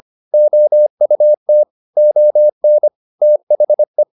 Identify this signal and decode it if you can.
Morse code, heard as a keyed tone, about 25 wpm, 600 Hz. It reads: OUT ON THE